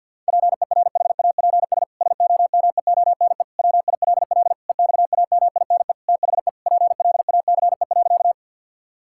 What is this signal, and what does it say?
Morse code, heard as a keyed tone, about 50 wpm, 700 Hz. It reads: PERHAPS SOMEONE WILL EXAMINE THE PLACE.
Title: Sentences from Top 1600 Words 50wpm